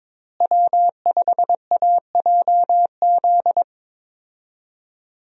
Morse code, heard as a keyed tone, about 22 wpm, 700 Hz. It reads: W5AJZ
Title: USA Call Signs - 5 Characters - Repeated 3x - 22wpm